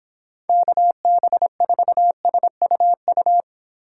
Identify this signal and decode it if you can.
Morse code, heard as a keyed tone, about 25 wpm, 700 Hz. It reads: KB4SUU
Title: Speed Racing - USA 2x3 Call Signs 22wpm